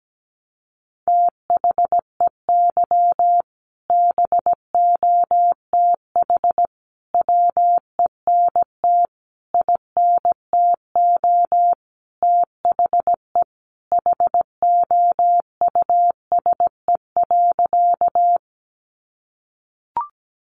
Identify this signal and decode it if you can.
Morse code, heard as a keyed tone, about 17 wpm, 700 Hz. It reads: THEY BOTH WENT INTO THE HOUSE.
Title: Sentences from Top 600 Words 17wpm